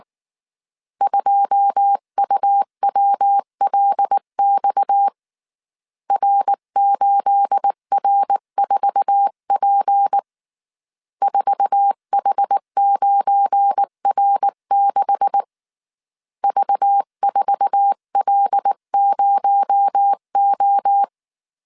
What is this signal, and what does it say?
Morse code, heard as a keyed tone, about 19 wpm, 800 Hz. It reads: 2UWLX R8R4P 4H9R6 V4L0O
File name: CW / DET / 1 9 1500C S B 18wpm 2551